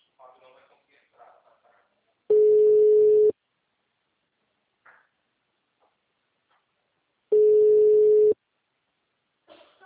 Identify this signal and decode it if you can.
Ringback tone (Europe, 425 Hz), with speech.
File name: out- -2001- - - .0